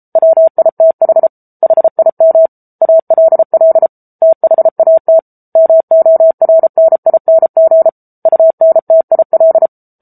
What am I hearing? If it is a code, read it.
Morse code, heard as a keyed tone, about 34 wpm, 650 Hz. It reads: WITH HIM ALL THAT MORNING UNTIL
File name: Book Two / 07-Chapter 7 - 33 WPM